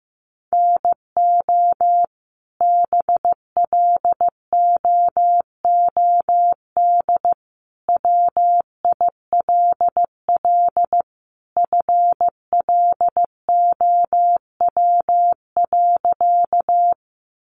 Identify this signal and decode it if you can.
Morse code, heard as a keyed tone, about 15 wpm, 700 Hz. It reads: NO BLOOD WILL FLOW.
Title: Sentences from Top 1200 Words 15wpm